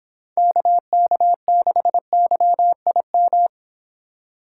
Morse code, heard as a keyed tone, about 24 wpm, 700 Hz. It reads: KK6YIM